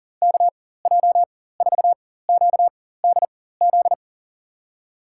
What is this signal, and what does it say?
Morse code, heard as a keyed tone, about 40 wpm, spaced as for 19 wpm, 700 Hz. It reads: KJ4QDZ
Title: USA Call Signs - 6 Characters - 4x Character Spacing - 40wpm